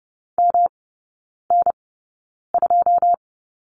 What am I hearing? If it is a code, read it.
Morse code, heard as a keyed tone, about 30 wpm, spaced as for 9 wpm, 700 Hz. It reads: MN2
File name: Lesson 18e - Letter m - Three Characters 30wpm